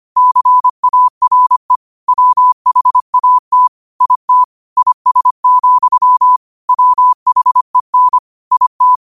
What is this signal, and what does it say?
Morse code, heard as a keyed tone, about 25 wpm, 1000 Hz. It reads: CARE WHAT IT IS, WHEN IT